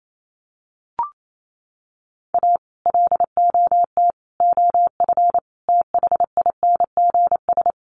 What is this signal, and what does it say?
Morse code, heard as a keyed tone, about 28 wpm, 700 Hz. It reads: A LOT OF THINGS
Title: Sets of 4 Words 28wpm